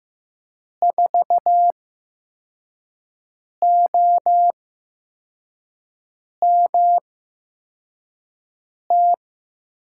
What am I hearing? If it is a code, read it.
Morse code, heard as a keyed tone, about 15 wpm, spaced as for 4 wpm, 700 Hz. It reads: 4OMT
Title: USA Call Signs - 6 Characters - 8x Character Spacing - 15wpm